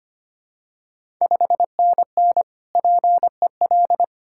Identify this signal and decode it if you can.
Morse code, heard as a keyed tone, about 25 wpm, 700 Hz. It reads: 5NN PEL
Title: Ontario QSO Party Exchange 25wpm